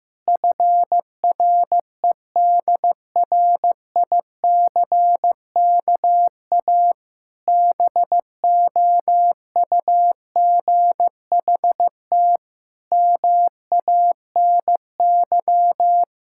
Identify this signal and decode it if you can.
Morse code, heard as a keyed tone, about 15 wpm, 700 Hz. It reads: FREDRICKA BOUGHT MANY
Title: Pangram Challenge v2 - 15wpm